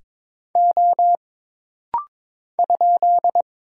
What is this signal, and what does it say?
Morse code, heard as a keyed tone, about 22 wpm, 700 Hz. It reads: O ?
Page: This rapid-fire timing is geared to push your ICR (Instant Character Recognition) skills.